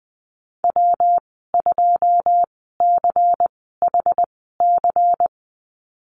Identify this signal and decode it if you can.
Morse code, heard as a keyed tone, about 20 wpm, spaced as for 14 wpm, 700 Hz. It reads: W2CHC
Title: 140 Lesson 23g - Letter f - Call Signs 20wpm